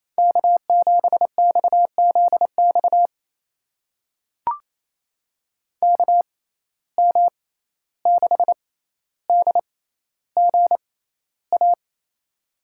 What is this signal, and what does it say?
Morse code, heard as a keyed tone, about 28 wpm, spaced as for 14 wpm, 700 Hz. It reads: K7XZX KM6DGA
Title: USA Call Signs - 6x Character Spacing - 28wpm